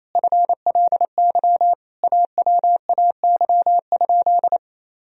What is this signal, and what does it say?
Morse code, heard as a keyed tone, about 28 wpm, 700 Hz. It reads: FLY AWAY?
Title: Sentences from Top 1000 Words - Review Part 2 - 28wpm